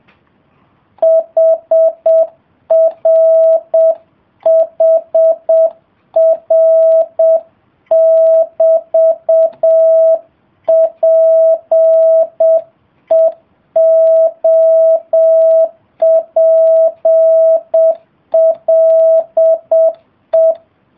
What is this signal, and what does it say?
Morse code, heard as a keyed tone, about 7 wpm, 650 Hz. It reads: HRHR<BT>PEOPLE